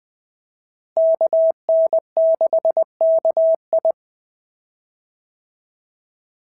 Morse code, heard as a keyed tone, about 18 wpm, 650 Hz. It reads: KN6KI
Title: POTA QSO Practice - 20wpm